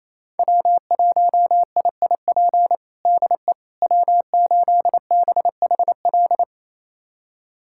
Morse code, heard as a keyed tone, about 28 wpm, 700 Hz. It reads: W1IIP DE W8BHL
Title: USA Callsign DE Callsign - 5 Characters - 28wpm